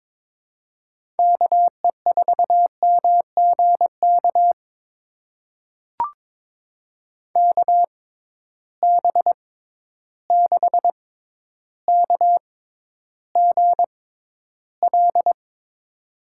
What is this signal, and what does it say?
Morse code, heard as a keyed tone, about 22 wpm, spaced as for 11 wpm, 700 Hz. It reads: KE4MGK KB6KGL
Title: USA Call Signs - 6 Characters - 6x Character Spacing - 22wpm